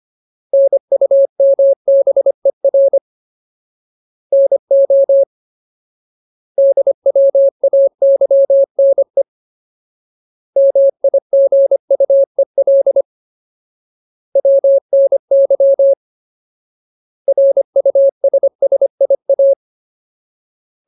Morse code, heard as a keyed tone, about 25 wpm, 550 Hz. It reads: NUMBER NO DWAYNE MIGUEL WNY RUSSIA
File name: Ultimate Word List / 25WPM Ultimate